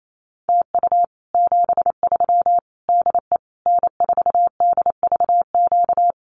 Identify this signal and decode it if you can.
Morse code, heard as a keyed tone, about 28 wpm, 700 Hz. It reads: TU 73 DE N4DVQ
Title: Common QSO Phrases Encore 28wpm